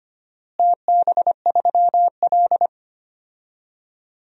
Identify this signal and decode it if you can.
Morse code, heard as a keyed tone, about 25 wpm, 700 Hz. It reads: TB3L
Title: DXCC-100 Call Signs - 4 Characters - Repeated 3x - 25wpm